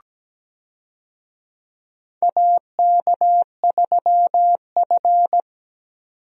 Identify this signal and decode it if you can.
Morse code, heard as a keyed tone, about 17 wpm, 700 Hz. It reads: AK3F
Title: USA Call Signs - 4 Characters - 17wpm